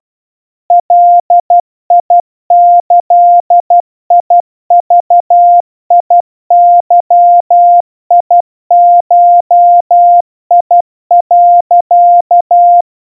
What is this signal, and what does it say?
Morse code (Japanese Wabun code), heard as a keyed tone, about 12 wpm, 700 Hz. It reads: ガギグゲゴ、
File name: kawa / Morse sample / iroha 40 cpm 700 Hz Crispy